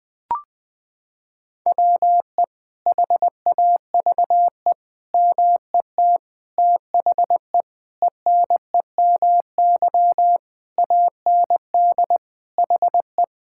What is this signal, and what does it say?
Morse code, heard as a keyed tone, about 21 wpm, 700 Hz. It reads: WE HAVE MET THE ENEMY AND HE
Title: Sentences with 9-Word Phrases - 20wpm